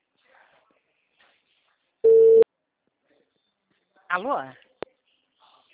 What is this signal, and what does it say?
Ringback tone (China, 450 Hz), with speech.
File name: out- -2008- - - .3470